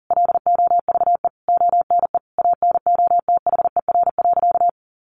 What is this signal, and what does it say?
Morse code, heard as a keyed tone, about 40 wpm, 700 Hz. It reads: LOVE ONE ANOTHER.
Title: Sentences from Top 600 Words 40wpm